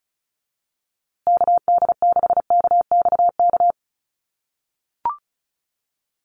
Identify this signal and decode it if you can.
Morse code, heard as a keyed tone, about 34 wpm, 700 Hz. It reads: KD6KXK
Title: USA Call Signs Repeated 2x 35wpm